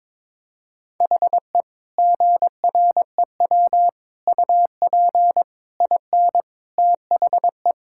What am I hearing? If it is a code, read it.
Morse code, heard as a keyed tone, about 22 wpm, 700 Hz. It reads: HE GREW UP IN THE
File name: practice / 22wpm / Sentences with 3-Word Phrases - 22wpm